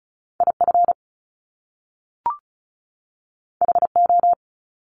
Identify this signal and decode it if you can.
Morse code, heard as a keyed tone, about 36 wpm, 700 Hz. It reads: IF HO
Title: Top Two Letter Combinations in Words 35wpm